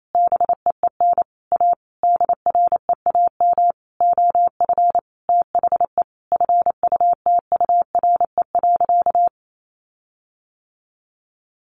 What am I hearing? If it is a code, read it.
Morse code, heard as a keyed tone, about 28 wpm, 700 Hz. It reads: BEEN A DREAM OF THE FUTURE.
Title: Sentences from Top 1500 Words 28wpm